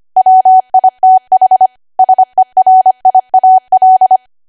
Morse code, heard as a keyed tone, about 25 wpm, 750 Hz. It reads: WITH SERIAL
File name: Archive / 25 WPM / 25WPM